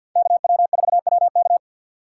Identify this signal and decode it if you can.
Morse code, heard as a keyed tone, about 52 wpm, 700 Hz. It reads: KW4WK
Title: USA Call Signs - 5 Characters - 6x Character Spacing - 50wpm